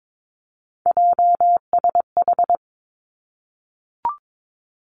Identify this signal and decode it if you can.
Morse code, heard as a keyed tone, about 22 wpm, 700 Hz. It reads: JSH
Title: Three Challenging Letter-Number Combinations 22wpm